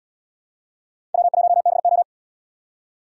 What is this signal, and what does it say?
Morse code, heard as a keyed tone, about 70 wpm, 700 Hz. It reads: W1BO